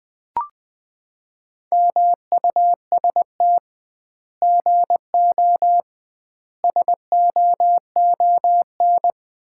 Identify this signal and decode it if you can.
Morse code, heard as a keyed tone, about 20 wpm, 700 Hz. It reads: MUST GO SOON
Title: Common QSO Phrases - 2x Word-Spacing - 20wpm